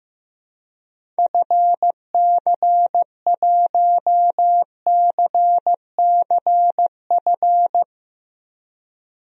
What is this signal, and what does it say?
Morse code, heard as a keyed tone, about 16 wpm, 700 Hz. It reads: FC1CCF
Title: DXCC-100 Call Signs - 6 Characters - 15wpm